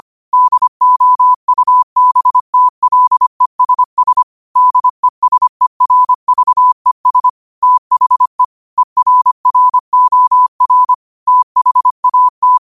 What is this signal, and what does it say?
Morse code, heard as a keyed tone, about 25 wpm, 1000 Hz. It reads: DOUBTLESS DESERVES THE ERROR THAT